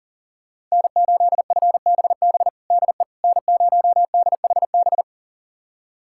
Morse code, heard as a keyed tone, about 40 wpm, 700 Hz. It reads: N8FBB DE N0DHB
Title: USA Callsign DE Callsign - 5 Characters - 40wpm